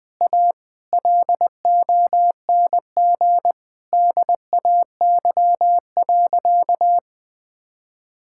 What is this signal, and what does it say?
Morse code, heard as a keyed tone, about 21 wpm, 700 Hz. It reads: A LONG DAY.
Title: Sentences with 7-Word Phrases - Spoken Beginning - 20wpm